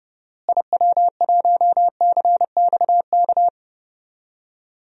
Morse code, heard as a keyed tone, about 30 wpm, 700 Hz. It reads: IW1CXK
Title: DXCC-100 Call Signs - 6 Characters - Repeated 2x - 30wpm